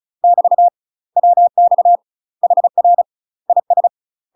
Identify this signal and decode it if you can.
Morse code, heard as a keyed tone, about 35 wpm, 700 Hz. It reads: <BT> WX HR IS
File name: UK MASTER QSOs TRAINING / 35WPM / UK MASTER QSOs 0041